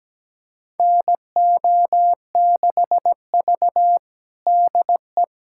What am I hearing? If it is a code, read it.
Morse code, heard as a keyed tone, about 17 wpm, 700 Hz. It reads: NO6V DE
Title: USA Callsign DE Callsign - 4 Characters - 17wpm